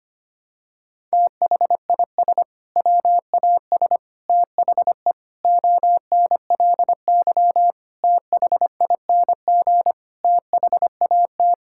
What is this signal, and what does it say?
Morse code, heard as a keyed tone, about 25 wpm, 700 Hz. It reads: THIS WAS THE ONLY THING THAT
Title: Sentences from Top 1000 Words - Review Part 6 - 25wpm